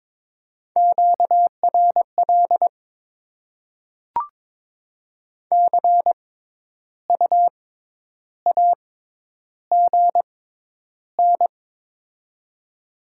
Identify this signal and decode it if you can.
Morse code, heard as a keyed tone, about 22 wpm, spaced as for 10 wpm, 700 Hz. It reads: QRL CU AGN
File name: Top 20 QSO Elements - 6x Character Spacing - 22wpm